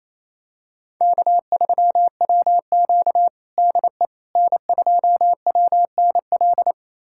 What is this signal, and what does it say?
Morse code, heard as a keyed tone, about 28 wpm, 700 Hz. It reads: K3WQ DE N2WNL
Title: USA Callsign DE Callsign 28wpm